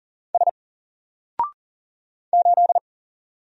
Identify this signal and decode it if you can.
Morse code, heard as a keyed tone, about 39 wpm, 700 Hz. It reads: S 8
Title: Single Character (Letter:Number:Punctuation:Prosign) 40wpm